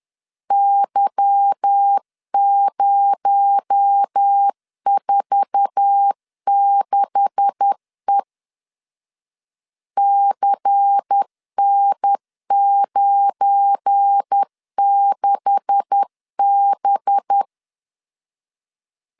Morse code, heard as a keyed tone, about 10 wpm, 800 Hz. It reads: Y046E CN96B
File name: LEZIONI / 9 286 10wpm 2735